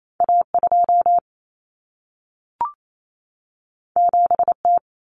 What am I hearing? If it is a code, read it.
Morse code, heard as a keyed tone, about 28 wpm, 700 Hz. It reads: A2 7T
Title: Common Cut Numbers - 2-Digit 28wpm